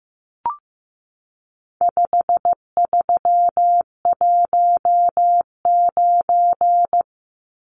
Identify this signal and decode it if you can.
Morse code, heard as a keyed tone, about 15 wpm, 700 Hz. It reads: 5319
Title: Numbers - 4-Digit 15wpm